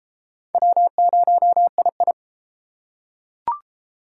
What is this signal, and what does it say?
Morse code, heard as a keyed tone, about 31 wpm, 700 Hz. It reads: W0II